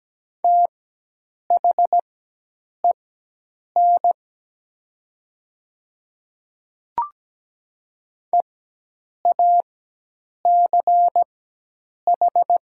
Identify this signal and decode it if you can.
Morse code, heard as a keyed tone, about 17 wpm, spaced as for 8 wpm, 700 Hz. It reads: THEN EACH
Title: 4x Character Spacing - Top 200 Words - 17wpm